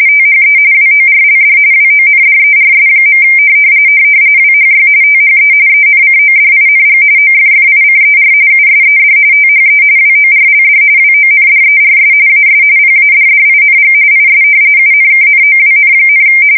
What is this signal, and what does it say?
RTTY, about 45.45 baud. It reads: THE WORLD WONDERS. WHY THE ENCODING OFFICER USED THE WORLD WONDERS AS PADDING IS UNKNOWN,
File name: fsk170 / battleshipnj